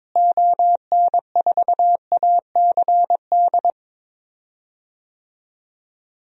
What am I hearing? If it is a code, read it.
Morse code, heard as a keyed tone, about 22 wpm, 700 Hz. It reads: ON4ACD
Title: DXCC-100 Call Signs - 6 Characters - Repeated 2x - 22wpm